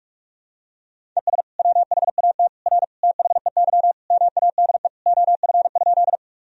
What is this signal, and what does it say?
Morse code, heard as a keyed tone, about 46 wpm, 700 Hz. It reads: ES WHAT R THEY MADE OF?
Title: Sentences from Top 200 Words and Abbreviations 45wpm